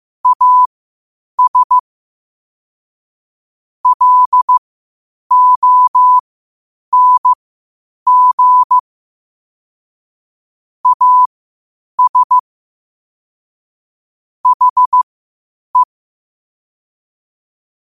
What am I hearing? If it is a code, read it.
Morse code, heard as a keyed tone, about 15 wpm, spaced as for 8 wpm, 1000 Hz. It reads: AS LONG AS HE
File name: qotdfiles / 8wpm